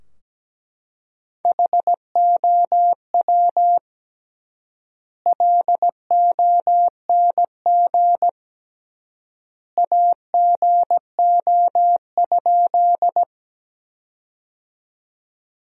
Morse code, heard as a keyed tone, about 17 wpm, 700 Hz. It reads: HOW LONG AGO?
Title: 3x Word-Spacing - Sentences from Top 500 Words - 17wpm